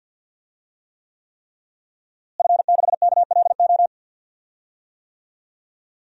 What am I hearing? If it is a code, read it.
Morse code, heard as a keyed tone, about 46 wpm, 700 Hz. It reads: F6KLO
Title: DXCC-100 Call Signs - 5 Characters - Repeated 3x - 50wpm